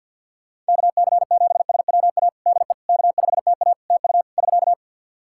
Morse code, heard as a keyed tone, about 51 wpm, 700 Hz. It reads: KC7SWA DE K5TA TU <SK>
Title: Practice QSO - Exchange 4 - Ending - 50wpm